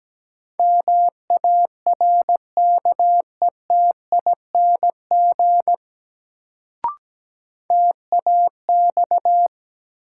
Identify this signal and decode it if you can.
Morse code, heard as a keyed tone, about 17 wpm, 700 Hz. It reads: MARKETING TAX
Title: Sentences from Top 1000 Words - Review Part 5 - New Word Drill - 17wpm